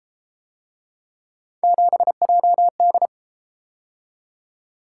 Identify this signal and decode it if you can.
Morse code, heard as a keyed tone, about 33 wpm, 700 Hz. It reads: 7JD